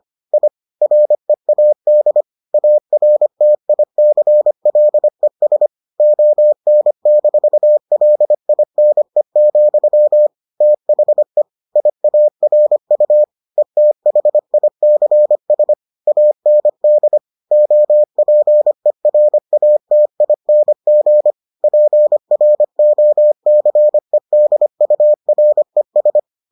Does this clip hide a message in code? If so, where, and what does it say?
Yes: Morse code from 0.3 to 26.2 s, heard as a keyed tone, about 25 wpm, 600 Hz. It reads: I READ ARTICLES ON-LINE, THE IARU ETHICS AND OPERATING PROCEDURES